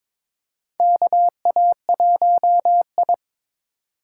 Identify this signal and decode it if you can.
Morse code, heard as a keyed tone, about 22 wpm, 700 Hz. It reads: KA1I
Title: USA Call Signs - 4x Character Spacing - 22wpm